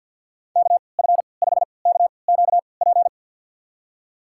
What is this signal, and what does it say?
Morse code, heard as a keyed tone, about 50 wpm, spaced as for 28 wpm, 700 Hz. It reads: KF5KQP
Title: USA Call Signs - 6 Characters - 3x Character Spacing - 50wpm